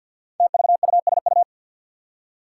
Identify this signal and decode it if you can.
Morse code, heard as a keyed tone, about 53 wpm, 700 Hz. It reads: TVUSU
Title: Five Challenging Letter-Number Combinations 50wpm